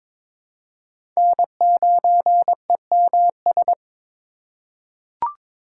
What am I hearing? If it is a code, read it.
Morse code, heard as a keyed tone, about 22 wpm, 700 Hz. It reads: N9EMS
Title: USA Call Signs - 8x Character Spacing - 22wpm